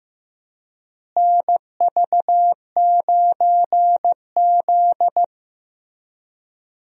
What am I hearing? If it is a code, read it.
Morse code, heard as a keyed tone, about 15 wpm, 700 Hz. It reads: NV9Z
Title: USA Call Signs - 4 Characters - Repeated 2x - 15wpm